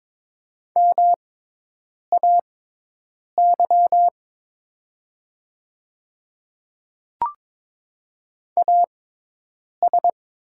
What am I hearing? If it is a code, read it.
Morse code, heard as a keyed tone, about 22 wpm, spaced as for 8 wpm, 700 Hz. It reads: MAY AS
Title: 6x Character Spacing - Top 100 Words - 22wpm